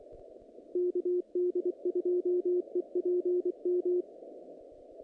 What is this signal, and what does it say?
Morse code, heard as a keyed tone, about 24 wpm, 360 Hz, with radio noise. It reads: KD2EPM
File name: contests / naqp-cw-k5tr-op-k5pi- -2200